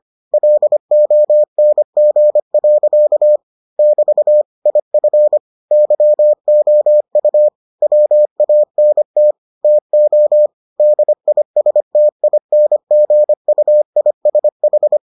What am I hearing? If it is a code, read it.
Morse code, heard as a keyed tone, about 25 wpm, 600 Hz. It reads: LONG. <BT> IF YOU WANT TO DISTINGUISH